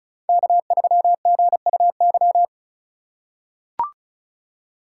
Morse code, heard as a keyed tone, about 34 wpm, 700 Hz. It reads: K3GUY
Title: USA Call Signs - 5 Characters - 3x Character Spacing - 35wpm